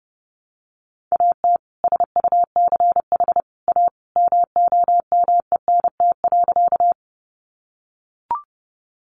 Morse code, heard as a keyed tone, about 30 wpm, 700 Hz. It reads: AT SUCH A MOMENT.
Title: Sentences from Top 1000 Words 30wpm